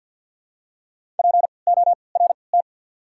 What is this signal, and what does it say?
Morse code, heard as a keyed tone, about 53 wpm, spaced as for 29 wpm, 700 Hz. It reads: PORT
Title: 3x Character Spacing - Top 500 Words - 50wpm